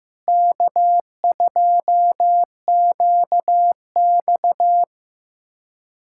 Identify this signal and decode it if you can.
Morse code, heard as a keyed tone, about 15 wpm, 700 Hz. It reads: K2QX
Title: USA Callsign DE Callsign - 4 Characters - 15wpm